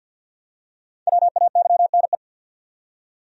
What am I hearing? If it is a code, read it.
Morse code, heard as a keyed tone, about 53 wpm, 700 Hz. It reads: WAYNE
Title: USA Common Names - Max 5 Letters - 50wpm